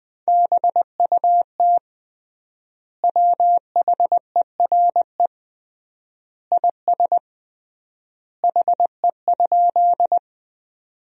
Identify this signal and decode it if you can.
Morse code, heard as a keyed tone, about 21 wpm, 700 Hz. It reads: BUT WHERE IS HE?
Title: 3x Word-Spacing - Sentences from Top 200 Words - 20wpm